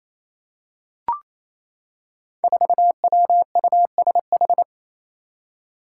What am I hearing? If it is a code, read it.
Morse code, heard as a keyed tone, about 27 wpm, 700 Hz. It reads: 4WUSH
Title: Five Challenging Letter-Number Combinations 28wpm